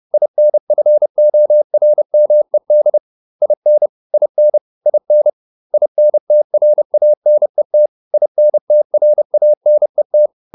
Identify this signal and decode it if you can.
Morse code, heard as a keyed tone, about 30 wpm, 600 Hz. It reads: INFORMED IN IN IN INTRANET INTRANET